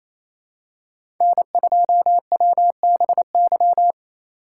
Morse code, heard as a keyed tone, about 28 wpm, 700 Hz. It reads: N2WBY
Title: USA Call Signs - 4x Character Spacing - 28wpm